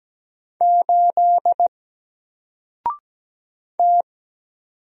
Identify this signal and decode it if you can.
Morse code, heard as a keyed tone, about 17 wpm, 700 Hz. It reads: 8 T
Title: Single Letter-Number Alt 17wpm